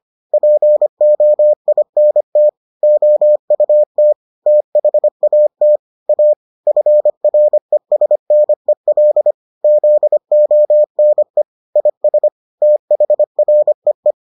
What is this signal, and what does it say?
Morse code, heard as a keyed tone, about 25 wpm, 600 Hz. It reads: POINT OUT THAT A FRESNEL ZONE IS THREE